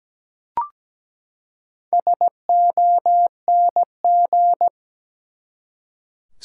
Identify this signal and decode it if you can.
Morse code, heard as a keyed tone, about 17 wpm, 700 Hz. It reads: SONG
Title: Four Letter Words - 17wpm